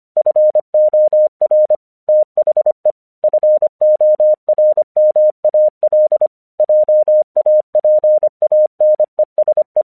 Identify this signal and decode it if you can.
Morse code, heard as a keyed tone, about 25 wpm, 610 Hz. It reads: FOR THE FORMAL JAPANESE
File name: 1 English 25wpm / AQC09